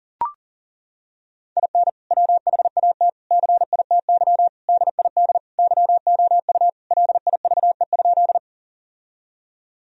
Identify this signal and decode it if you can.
Morse code, heard as a keyed tone, about 40 wpm, 700 Hz. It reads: IN WHAT CITY DID YOU LIVE?
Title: Sentences from Top 1000 Words - Review Part 1 - 40wpm